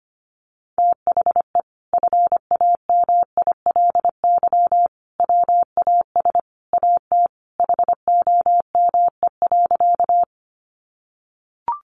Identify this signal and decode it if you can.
Morse code, heard as a keyed tone, about 25 wpm, 700 Hz. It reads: THE FAMILY WAS AT HOME.
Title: Sentences from Top 400 Words 25wpm